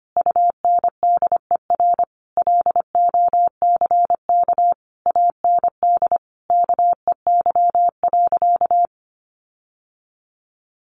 Morse code, heard as a keyed tone, about 25 wpm, 700 Hz. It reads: UNDER LOCK AND KEY.
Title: Sentences from Top 1400 Words 25wpm